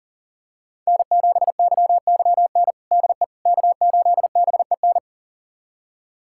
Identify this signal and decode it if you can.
Morse code, heard as a keyed tone, about 40 wpm, 700 Hz. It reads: N7YYN DE K8BEN
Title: USA Callsign DE Callsign - 5 Characters - 40wpm